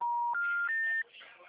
Special information tone, with noise.
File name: out- -2019- - - .83733